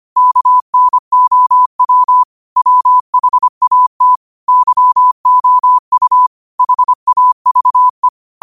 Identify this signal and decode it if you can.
Morse code, heard as a keyed tone, about 25 wpm, 1000 Hz. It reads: KNOW WHAT YOU HAVE